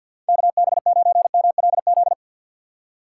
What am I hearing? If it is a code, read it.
Morse code, heard as a keyed tone, about 52 wpm, 700 Hz. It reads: KB9MLZ
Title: USA Call Signs - 6 Characters - 4x Character Spacing - 50wpm